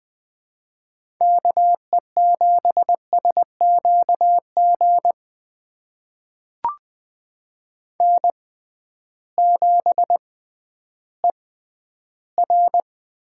Morse code, heard as a keyed tone, about 20 wpm, spaced as for 12 wpm, 700 Hz. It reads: KE7SQG N7ER
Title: USA Call Signs - 6x Character Spacing - 20wpm